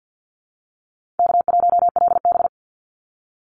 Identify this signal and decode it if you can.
Morse code, heard as a keyed tone, about 50 wpm, 700 Hz. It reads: K1LB
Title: Easy Call Signs Repeated 3x - 50wpm